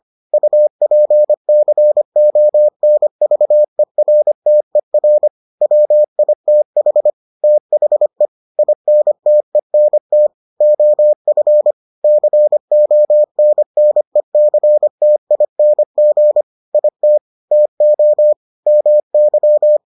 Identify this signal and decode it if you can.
Morse code, heard as a keyed tone, about 25 wpm, 600 Hz. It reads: UPCONVERTER WITH THE INTENT OF CONNECTING IT TO MY